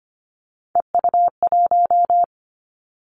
Morse code, heard as a keyed tone, about 25 wpm, 700 Hz. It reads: EU1
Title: Callsign Prefixes for Top 50 DXCC - 25wpm